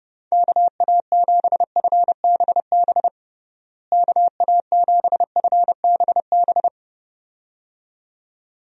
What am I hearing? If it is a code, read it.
Morse code, heard as a keyed tone, about 30 wpm, 700 Hz. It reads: KA7FBB KA7FBB
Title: USA Call Signs - 6 Characters - Repeated 2x - 30wpm